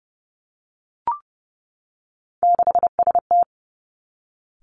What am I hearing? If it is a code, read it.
Morse code, heard as a keyed tone, about 30 wpm, 700 Hz. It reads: 6ST
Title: Three Challenging Letter-Number Combinations 30wpm